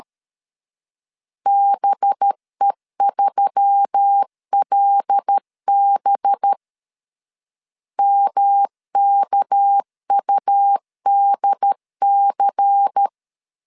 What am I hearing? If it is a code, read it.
Morse code, heard as a keyed tone, about 13 wpm, 800 Hz. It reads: BE3LB MKUDC
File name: LEZIONI / 6 PKX0 / 6 PK 12wpm 2006